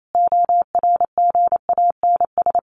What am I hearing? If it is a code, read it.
Morse code, heard as a keyed tone, about 28 wpm, 700 Hz. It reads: ORGANS
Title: A Sound Of Thunder 28wpm